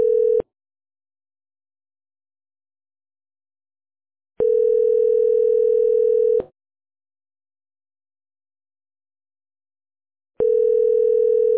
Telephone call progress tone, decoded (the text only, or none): ringback tone